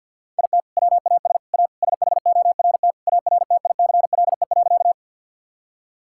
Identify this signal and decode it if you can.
Morse code, heard as a keyed tone, about 46 wpm, 700 Hz. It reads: IT WAS A SHORT ARTICLE.
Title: Sentences from Top 1000 Words - Review Part 5 - 50wpm